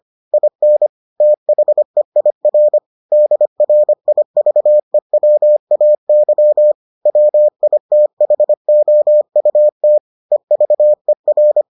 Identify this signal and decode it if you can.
Morse code, heard as a keyed tone, about 25 wpm, 600 Hz. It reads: IN THEIR DRIVEWAY WITHOUT EVER